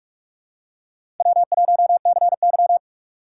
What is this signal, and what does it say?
Morse code, heard as a keyed tone, about 44 wpm, 700 Hz. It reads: W1CY
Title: USA Call Signs - 6x Character Spacing - 45wpm